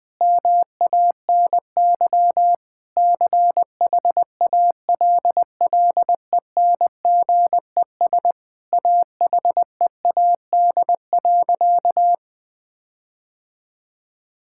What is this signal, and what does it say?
Morse code, heard as a keyed tone, about 20 wpm, 700 Hz. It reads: MANY CHALLENGES AHEAD.
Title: Sentences with 5-Word Phrases - 20wpm